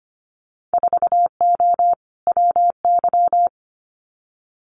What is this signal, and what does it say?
Morse code, heard as a keyed tone, about 25 wpm, 690 Hz. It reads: 4O WY
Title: Winter Field Day Exchanges Encore 25wpm